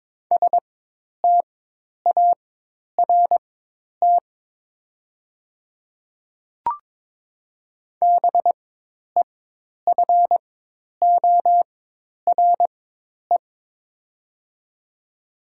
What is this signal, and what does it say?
Morse code, heard as a keyed tone, about 22 wpm, spaced as for 10 wpm, 700 Hz. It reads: START BEFORE
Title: 4x Character Spacing - Top 300 Words - 22wpm